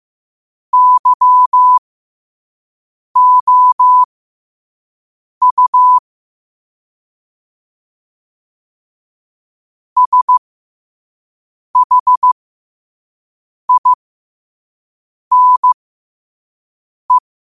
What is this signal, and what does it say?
Morse code, heard as a keyed tone, about 15 wpm, spaced as for 5 wpm, 1000 Hz. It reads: YOU SHINE
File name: qotdfiles / 5wpm